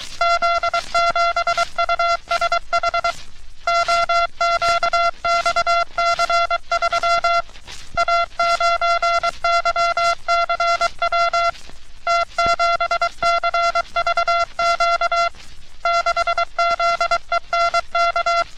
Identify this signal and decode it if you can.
Morse code, heard as a keyed tone, about 23 wpm, 1370 Hz. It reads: Z7USH OQXC3 A9YCW T7CVQ 6ZENK
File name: U.S. Navy - RN-2849-124 - Sixteens Morse Code Practice